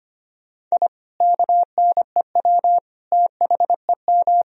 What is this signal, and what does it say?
Morse code, heard as a keyed tone, about 25 wpm, 700 Hz. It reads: I KNEW THEM
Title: Sentences from Top 1000 Words - Review Part 1 - 25wpm